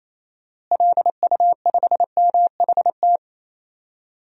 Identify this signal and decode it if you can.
Morse code, heard as a keyed tone, about 28 wpm, 700 Hz. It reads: LU5MHT
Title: DXCC-100 Call Signs - 6 Characters - Repeated 2x - 28wpm